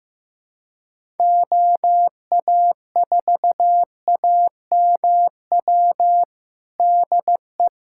Morse code, heard as a keyed tone, about 15 wpm, 700 Hz. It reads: OA4AMW DE
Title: USA Callsign DE Callsign 15wpm